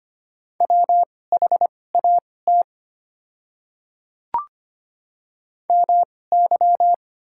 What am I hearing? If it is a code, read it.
Morse code, heard as a keyed tone, about 25 wpm, spaced as for 18 wpm, 690 Hz. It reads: WHAT MY
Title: 2x Character Spacing - Top 100 Words - 25wpm